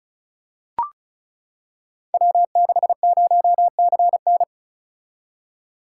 Morse code, heard as a keyed tone, about 36 wpm, 690 Hz. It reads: W60CN
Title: Code Groups - 5 Letter:Numbers - 35wpm